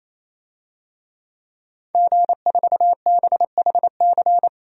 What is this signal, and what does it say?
Morse code, heard as a keyed tone, about 27 wpm, 700 Hz. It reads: G4BHC
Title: DXCC-100 Call Signs - 5 Characters - 28wpm